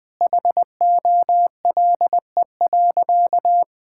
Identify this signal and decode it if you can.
Morse code, heard as a keyed tone, about 21 wpm, 700 Hz. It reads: HOLE.
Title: Sentences from Top 1600 Words 20wpm